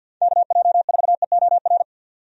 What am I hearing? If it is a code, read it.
Morse code, heard as a keyed tone, about 51 wpm, 700 Hz. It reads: KJ4EOR
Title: USA Call Signs - 6 Characters - 50wpm